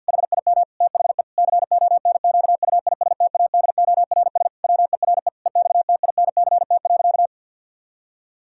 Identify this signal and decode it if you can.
Morse code, heard as a keyed tone, about 50 wpm, 700 Hz. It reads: HIM THE CONQUISTADORS WERE EXTINCT.
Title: Pangram Challenge v2 - 50wpm